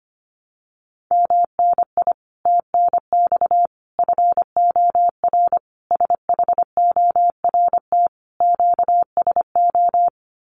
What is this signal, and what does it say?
Morse code, heard as a keyed tone, about 25 wpm, 700 Hz. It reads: MNI TNX FOR SHORT QSO
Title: Common QSO Phrases 25wpm